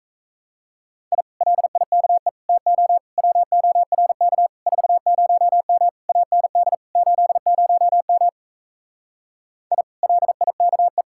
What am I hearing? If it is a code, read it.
Morse code, heard as a keyed tone, about 38 wpm, 700 Hz. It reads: I LIKE TO WORK 40M AND 80M I LIKE